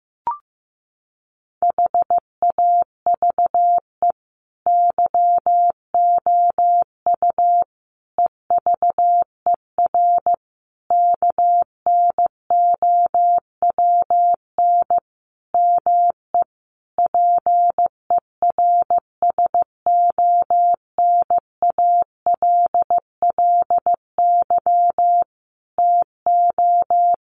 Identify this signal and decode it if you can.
Morse code, heard as a keyed tone, about 14 wpm, 700 Hz. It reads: HAVE YOU EVER KNOWN ME PERSONALLY TO
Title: Sentences from Top 1400 Words 15wpm